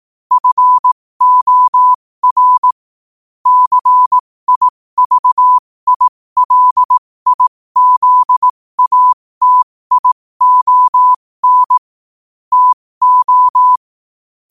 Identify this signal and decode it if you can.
Morse code, heard as a keyed tone, about 18 wpm, spaced as for 15 wpm, 1000 Hz. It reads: FOR CIVILIZATION TO
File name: qotdfiles / 15wpm